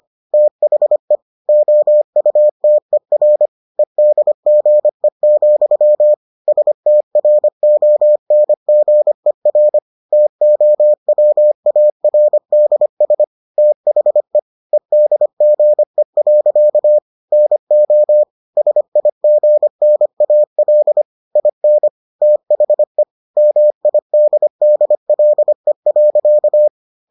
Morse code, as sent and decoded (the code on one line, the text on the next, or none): - .... / --- ..- - . .-. / . -.. --. --..-- / ... - .-. --- -. --. .-. / - --- .-- .- .-. -.. ... / - .... / . -.. --. .-.-.- / -. --- / ... .. --. -. .- .-.. / .. -. / - .... / -- .. -.. -.. .-.. .-.-.-
THE OUTER EDGE, STRONGER TOWARDS THE EDGE. NO SIGNAL IN THE MIDDLE.